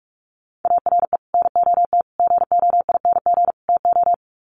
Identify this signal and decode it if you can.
Morse code, heard as a keyed tone, about 44 wpm, 700 Hz. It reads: ARE NOT GOING TO
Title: Sets of 4 Words 45wpm